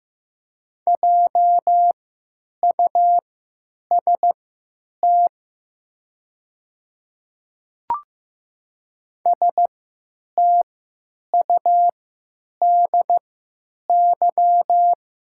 Morse code, heard as a keyed tone, about 15 wpm, spaced as for 9 wpm, 700 Hz. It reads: JUST STUDY
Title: 3x Character Spacing - Top 300 Words - 15wpm